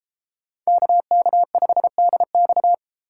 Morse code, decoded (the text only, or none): KK5DX